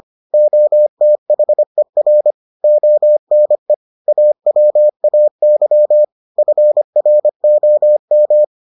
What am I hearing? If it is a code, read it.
Morse code, heard as a keyed tone, about 25 wpm, 600 Hz. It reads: OTHER ONE AWAY FROM